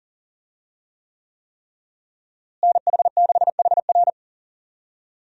Morse code, heard as a keyed tone, about 37 wpm, 700 Hz. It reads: NH6HR
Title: USA Call Signs - 5 Characters - Repeated 3x - 40wpm